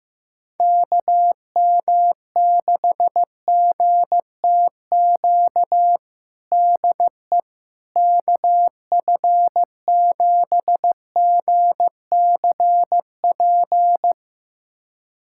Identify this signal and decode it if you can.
Morse code, heard as a keyed tone, about 16 wpm, 700 Hz. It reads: KM6GTQ DE KF7GCP
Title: USA Callsign DE Callsign - 6 Characters - 15wpm